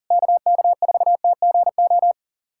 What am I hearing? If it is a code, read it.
Morse code, heard as a keyed tone, about 44 wpm, 700 Hz. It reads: KK4TGO
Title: SOTA QSO Practice 40wpm